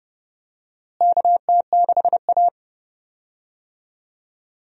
Morse code, heard as a keyed tone, about 32 wpm, 700 Hz. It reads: KT6A
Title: USA Call Signs - 4 Characters - Repeated 2x - 30wpm